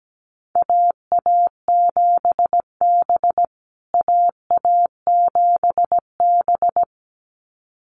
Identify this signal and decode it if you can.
Morse code, heard as a keyed tone, about 17 wpm, 700 Hz. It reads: AA7B AA7B
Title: USA Call Signs Repeated 2x 17wpm